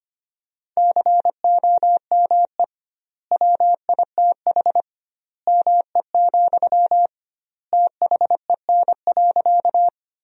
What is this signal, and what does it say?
Morse code, heard as a keyed tone, about 25 wpm, 700 Hz. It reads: COME WITH ME, THEN.
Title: 2x Word-Spacing - Sentences from Top 200 Words - 25wpm